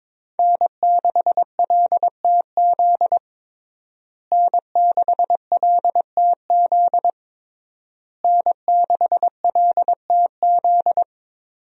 Morse code, heard as a keyed tone, about 21 wpm, 700 Hz. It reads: N6LTZ N6LTZ N6LTZ
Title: USA Call Signs - 5 Characters - Repeated 3x - 22wpm